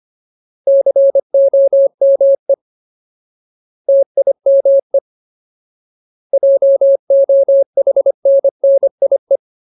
Morse code, heard as a keyed tone, about 25 wpm, 550 Hz. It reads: COME TIME JOHNNIE
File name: Ultimate Word List / 25WPM Ultimate